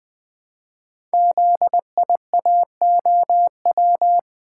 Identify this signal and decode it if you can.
Morse code, heard as a keyed tone, about 20 wpm, 700 Hz. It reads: ZIAOW
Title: Code Groups - 5 Letter:Numbers - 20wpm